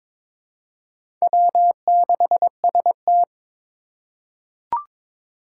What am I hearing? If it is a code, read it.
Morse code, heard as a keyed tone, about 22 wpm, 700 Hz. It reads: W6ST
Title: USA Call Signs - 4 Characters - 2x Character Spacing - 22wpm